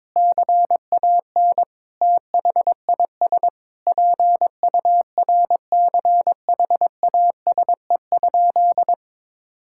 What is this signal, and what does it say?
Morse code, heard as a keyed tone, about 22 wpm, 700 Hz. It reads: CAN THIS PURCHASE?
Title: Sentences from Top 1000 Words - Review Part 5 - 22wpm